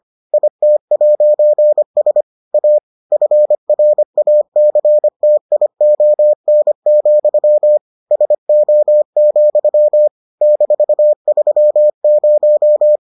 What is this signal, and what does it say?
Morse code, heard as a keyed tone, about 25 wpm, 600 Hz. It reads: IT'S A FRACTION, SO, -30